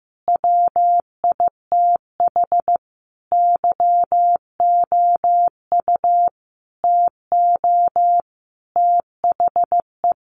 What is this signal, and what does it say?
Morse code, heard as a keyed tone, about 14 wpm, 700 Hz. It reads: WITH YOU TO THE
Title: Sentences from Top 1400 Words 15wpm